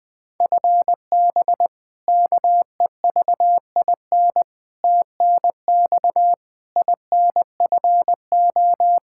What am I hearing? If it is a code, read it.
Morse code, heard as a keyed tone, about 20 wpm, 700 Hz. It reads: FB KEVIN TNX INFO
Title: Practice QSO - Exchange 3 - More Info - 20wpm